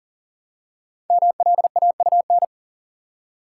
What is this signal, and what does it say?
Morse code, heard as a keyed tone, about 40 wpm, 700 Hz. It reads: MLAUN
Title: Code Groups - 5 Letters - 40wpm